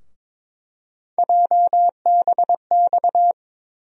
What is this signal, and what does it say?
Morse code, heard as a keyed tone, about 21 wpm, 700 Hz. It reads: JBX
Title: USA Call Sign Postfixes - 3 Letters 22wpm